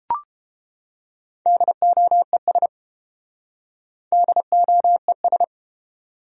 Morse code, heard as a keyed tone, about 29 wpm, 700 Hz. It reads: DOES DOES